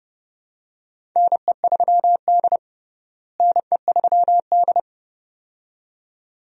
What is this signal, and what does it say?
Morse code, heard as a keyed tone, about 31 wpm, 700 Hz. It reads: NE3D NE3D
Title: USA Call Signs - 4 Characters - Repeated 2x - 30wpm